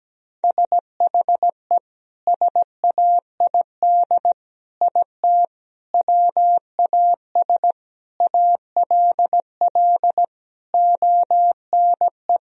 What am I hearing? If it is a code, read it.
Morse code, heard as a keyed tone, about 17 wpm, 700 Hz. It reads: SHE SAID IT WAS ALL ONE
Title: Sentences from Top 1000 Words - Review Part 2 - 17wpm